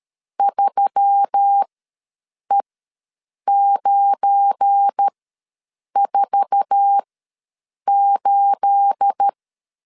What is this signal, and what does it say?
Morse code, heard as a keyed tone, about 13 wpm, spaced as for 7 wpm, 800 Hz. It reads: 3E948
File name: LEZIONI / 9 28 ws 12wpm 3354